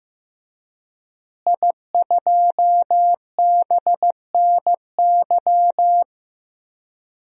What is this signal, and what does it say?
Morse code, heard as a keyed tone, about 15 wpm, 700 Hz. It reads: I2BNY
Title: DXCC-100 Call Signs - 5 Characters - Repeated 3x - 15wpm